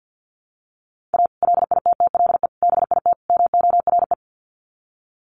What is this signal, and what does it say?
Morse code, heard as a keyed tone, about 50 wpm, 700 Hz. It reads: A LITTLE BIT MORE
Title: Sets of 4 Words 50wpm